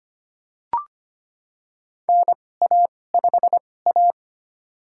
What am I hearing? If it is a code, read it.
Morse code, heard as a keyed tone, about 25 wpm, spaced as for 18 wpm, 700 Hz. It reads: NA5A
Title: USA Call Signs - 4 Characters - 2x Character Spacing - 25wpm